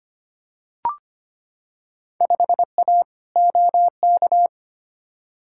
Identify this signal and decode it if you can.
Morse code, heard as a keyed tone, about 25 wpm, 700 Hz. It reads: 5A OK
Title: Field Day Exchanges - 25wpm